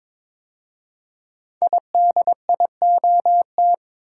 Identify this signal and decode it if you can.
Morse code, heard as a keyed tone, about 22 wpm, 690 Hz. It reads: IDIOT
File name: 5 Letter Words - First 3 Letters Given - 22wpm